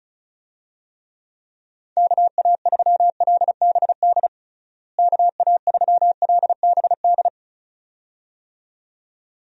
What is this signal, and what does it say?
Morse code, heard as a keyed tone, about 36 wpm, 700 Hz. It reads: KA3LBD KA3LBD
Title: USA Call Signs - 6 Characters - Repeated 2x - 35wpm